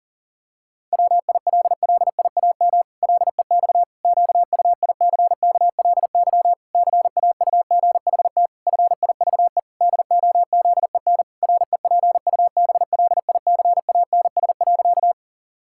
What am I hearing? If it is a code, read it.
Morse code, heard as a keyed tone, about 40 wpm, 700 Hz. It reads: WILLIAM LEX QUICKLY CAUGHT FIVE DOZEN REPUBLICANS.
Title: Pangram Challenge v2 - 40wpm